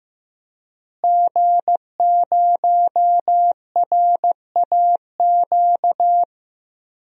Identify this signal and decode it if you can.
Morse code, heard as a keyed tone, about 15 wpm, 700 Hz. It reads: G0RAQ
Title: DXCC-100 Call Signs - 5 Characters - Repeated 2x - 15wpm